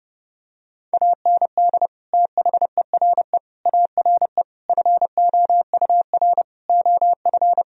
Morse code, heard as a keyed tone, about 30 wpm, 700 Hz. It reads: AND THERE ARE FOUR OF
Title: Sentences from Top 1000 Words - Review Part 1 - 30wpm